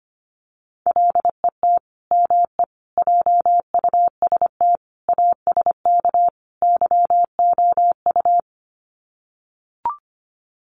Morse code, heard as a keyed tone, about 25 wpm, 700 Hz. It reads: LET ME JUST ASK YOU
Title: Sets of 5 Words Encore2 25wpm